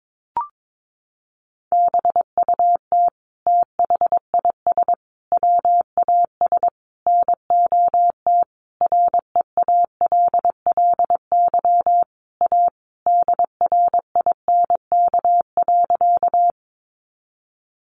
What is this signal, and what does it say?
Morse code, heard as a keyed tone, about 22 wpm, 700 Hz. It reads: BUT THIS WAS NOT REALLY A DRINK.
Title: Sentences from Top 1500 Words 22wpm